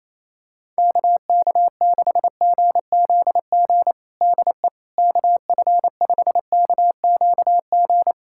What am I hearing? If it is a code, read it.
Morse code, heard as a keyed tone, about 28 wpm, 700 Hz. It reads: KK6GZG DE KF5KQG
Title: USA Callsign DE Callsign - 6 Characters - 28wpm